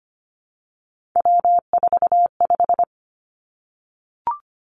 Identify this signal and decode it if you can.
Morse code, heard as a keyed tone, about 25 wpm, 700 Hz. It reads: W45
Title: Three Challenging Letter-Number Combinations 25wpm